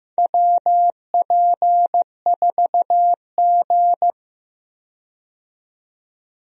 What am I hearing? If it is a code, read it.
Morse code, heard as a keyed tone, about 15 wpm, 700 Hz. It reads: WP4G
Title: USA Call Signs - 4 Characters - 15wpm